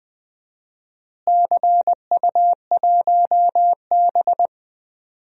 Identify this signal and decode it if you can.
Morse code, heard as a keyed tone, about 20 wpm, 700 Hz. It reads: CU1B
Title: DXCC-100 Call Signs - 4 Characters - Repeated 3x - 20wpm